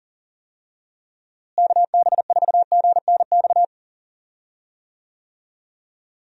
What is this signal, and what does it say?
Morse code, heard as a keyed tone, about 40 wpm, 700 Hz. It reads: KB4GNX
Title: USA Call Signs - 6 Characters - Repeated 2x - 40wpm